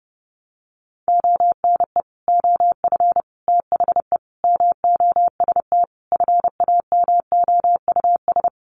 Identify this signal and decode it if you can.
Morse code, heard as a keyed tone, about 30 wpm, 700 Hz. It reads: ONE OF THE MOST FAMOUS
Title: Sets of 5 Words Encore2 30wpm